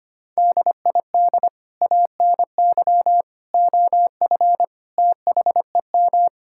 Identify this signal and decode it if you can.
Morse code, heard as a keyed tone, about 25 wpm, 700 Hz. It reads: DID ANY OF THEM
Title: Sentences from Top 1600 Words 25wpm